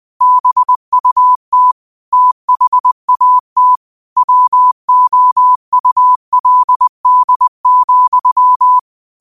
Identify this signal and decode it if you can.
Morse code, heard as a keyed tone, about 20 wpm, 1000 Hz. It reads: BUT THAT WOULD,
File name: qotdfiles / 20wpm